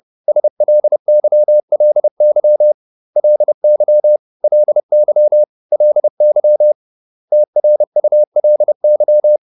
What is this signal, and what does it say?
Morse code, heard as a keyed tone, about 30 wpm, 600 Hz. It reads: SLYLY LY LY LY TRULY